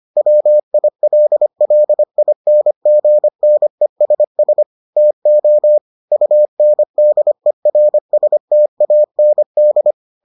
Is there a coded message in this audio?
Yes: Morse code from 0.2 to 9.9 s, heard as a keyed tone, about 25 wpm, 600 Hz. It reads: WILLINGNESS TO UNDERSTAND